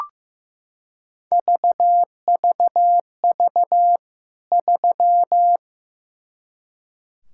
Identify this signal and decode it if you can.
Morse code, heard as a keyed tone, about 15 wpm, 700 Hz. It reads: VVV 3
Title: K1USN SST QSO Practice 15wpm